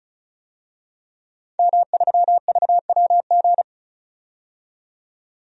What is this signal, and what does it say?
Morse code, heard as a keyed tone, about 34 wpm, 700 Hz. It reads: M3VWG
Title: DXCC-100 Call Signs - 5 Characters - 35wpm